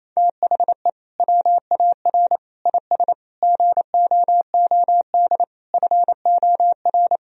Morse code, heard as a keyed tone, about 28 wpm, 700 Hz. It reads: THE WAR IS GOOD FOR
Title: Sentences from Top 1000 Words - Review Part 4 - 28wpm